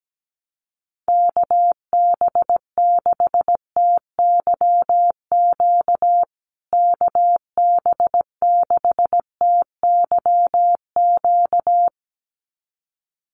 Morse code, heard as a keyed tone, about 17 wpm, 700 Hz. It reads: KB6TYQ KB6TYQ
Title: USA Call Signs Repeated 2x 17wpm